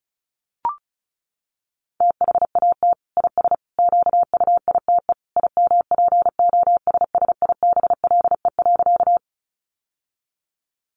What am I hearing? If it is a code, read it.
Morse code, heard as a keyed tone, about 34 wpm, 700 Hz. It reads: THAT IS QUITE IMPOSSIBLE.
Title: Sentences from Top 1100 Words 35wpm